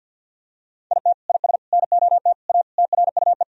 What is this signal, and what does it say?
Morse code, heard as a keyed tone, about 50 wpm, 700 Hz. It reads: IT IS NOT A TRUE
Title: Sentences from Top 1000 Words - Review Part 4 - 50wpm